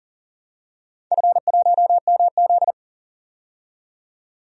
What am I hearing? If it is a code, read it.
Morse code, heard as a keyed tone, about 40 wpm, 700 Hz. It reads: F1MZ
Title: DXCC-100 Call Signs - 4 Characters - 40wpm